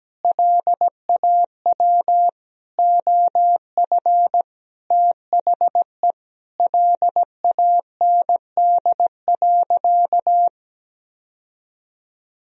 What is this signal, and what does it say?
Morse code, heard as a keyed tone, about 17 wpm, 700 Hz. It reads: LAW OF THE LAND.
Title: Sentences from Top 1000 Words - Review Part 3 - 17wpm